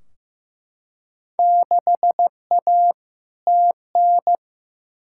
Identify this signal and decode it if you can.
Morse code, heard as a keyed tone, about 15 wpm, 700 Hz. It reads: 6A TN
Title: Field Day Exchanges Encore - 15wpm